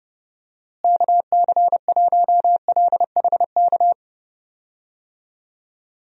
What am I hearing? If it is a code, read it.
Morse code, heard as a keyed tone, about 30 wpm, 700 Hz. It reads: KC1LHK
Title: USA Call Signs - 6 Characters - Repeated 3x - 30wpm